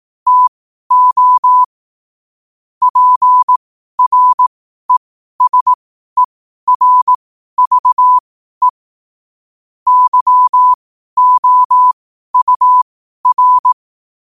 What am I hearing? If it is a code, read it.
Morse code, heard as a keyed tone, about 18 wpm, spaced as for 12 wpm, 1000 Hz. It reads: TO PRESERVE YOUR
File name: qotdfiles / 12wpm